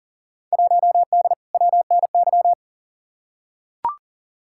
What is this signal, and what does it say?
Morse code, heard as a keyed tone, about 40 wpm, 700 Hz. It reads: 1D WNY
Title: Field Day Exchanges - 40wpm